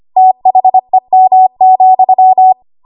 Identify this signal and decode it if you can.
Morse code, heard as a keyed tone, about 25 wpm, 750 Hz. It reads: THEM,
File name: Archive / 25 WPM / 25WPM